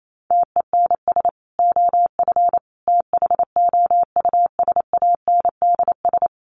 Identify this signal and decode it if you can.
Morse code, heard as a keyed tone, about 28 wpm, 700 Hz. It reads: TENS OF THOUSANDS
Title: Sets of 5 Words Encore2 28wpm